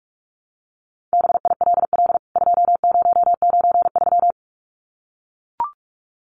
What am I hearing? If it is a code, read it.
Morse code, heard as a keyed tone, about 44 wpm, 700 Hz. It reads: BILL 2093
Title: CWT Exchange 45wpm